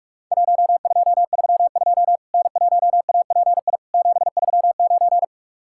Morse code, heard as a keyed tone, about 44 wpm, 700 Hz. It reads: 1232 N1API 739
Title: Worked All Europe DX Contest - QTC Exchange - 45wpm